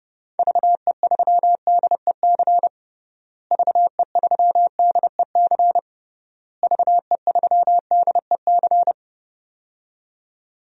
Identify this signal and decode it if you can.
Morse code, heard as a keyed tone, about 30 wpm, 700 Hz. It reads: VE3DEC VE3DEC VE3DEC
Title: DXCC-100 Call Signs - 6 Characters - Repeated 3x - 30wpm